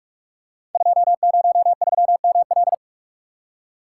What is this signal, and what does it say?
Morse code, heard as a keyed tone, about 44 wpm, 700 Hz. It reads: 203ML
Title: Code Groups - 5 Letter:Numbers - 45wpm